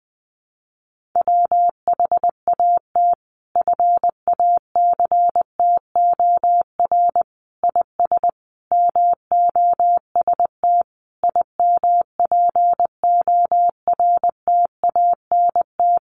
Morse code, heard as a keyed tone, about 20 wpm, 700 Hz. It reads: WHAT FACTOR IS MOST IMPORTANT
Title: Sentences from Top 1300 Words 20wpm